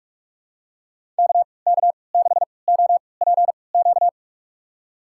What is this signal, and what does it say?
Morse code, heard as a keyed tone, about 45 wpm, spaced as for 25 wpm, 690 Hz. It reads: KK6OPQ
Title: USA Call Signs - 6 Characters - 3x Character Spacing - 45wpm